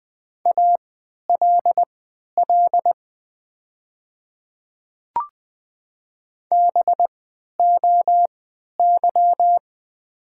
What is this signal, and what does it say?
Morse code, heard as a keyed tone, about 20 wpm, spaced as for 11 wpm, 700 Hz. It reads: ALL BOY
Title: 3x Character Spacing - Top 300 Words - 20wpm